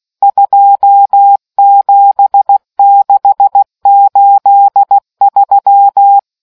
Morse code, heard as a keyed tone, about 16 wpm, 800 Hz. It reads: 27683